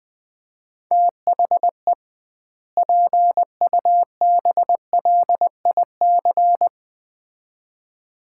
Morse code, heard as a keyed tone, about 21 wpm, 700 Hz. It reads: THE PUBLIC
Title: 2x Word-Spacing - Sets of 2 Words - 20wpm